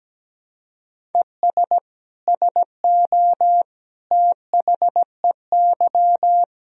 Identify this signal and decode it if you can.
Morse code, heard as a keyed tone, about 17 wpm, 700 Hz. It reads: ES SO THEY
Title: Sentences from Top 400 Words and Abbreviations 17wpm